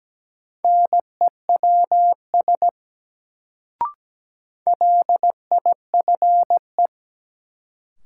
Morse code, heard as a keyed tone, about 17 wpm, 700 Hz. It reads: NEWS LIFE
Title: Sets of 5 Words - New Word Drill - 17wpm